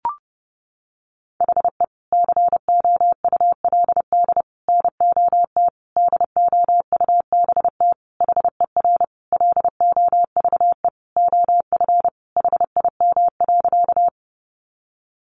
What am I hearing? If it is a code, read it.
Morse code, heard as a keyed tone, about 30 wpm, 700 Hz. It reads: HE COULD NOT DOUBT HER LOVE OF HIM.
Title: Sentences from Top 1500 Words 30wpm